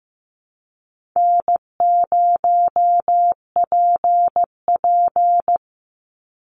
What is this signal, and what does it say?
Morse code, heard as a keyed tone, about 15 wpm, 700 Hz. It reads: N0PP
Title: Easy Call Signs Repeated 2x - 15wpm